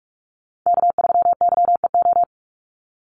Morse code, heard as a keyed tone, about 44 wpm, 700 Hz. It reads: K3YEO
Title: Call Signs Repeated 2x - 45wpm